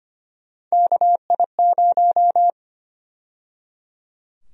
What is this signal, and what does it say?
Morse code, heard as a keyed tone, about 25 wpm, 700 Hz. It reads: KI0
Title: USA Call Sign Prefixes - 25wpm